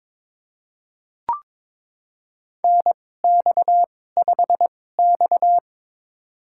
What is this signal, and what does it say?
Morse code, heard as a keyed tone, about 22 wpm, spaced as for 16 wpm, 700 Hz. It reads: NX5X
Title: USA Call Signs - 4 Characters - 2x Character Spacing - 22wpm